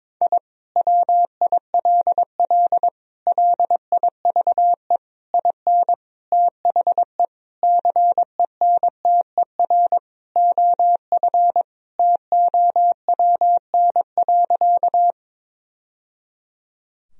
Morse code, heard as a keyed tone, about 22 wpm, 700 Hz. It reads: I WILL LIVE IN THE CENTER OF TOWN.
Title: Sentences from Top 1000 Words - Review Part 2 - 22wpm